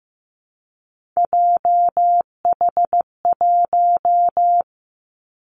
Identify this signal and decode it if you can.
Morse code, heard as a keyed tone, about 15 wpm, 700 Hz. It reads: JH1
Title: Callsign Prefixes for Top 25 DXCC - 15wpm